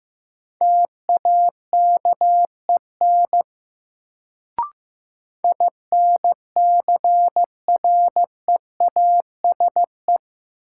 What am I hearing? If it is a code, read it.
Morse code, heard as a keyed tone, about 15 wpm, 700 Hz. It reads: TAKEN INCREASE
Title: Sentences from Top 1000 Words - Review Part 4 - New Word Drill - 15wpm